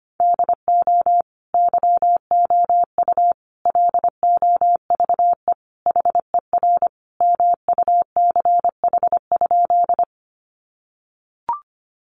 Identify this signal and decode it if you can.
Morse code, heard as a keyed tone, about 25 wpm, 700 Hz. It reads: DO YOU LOVE HER MUCH?
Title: Sentences from Top 400 Words 25wpm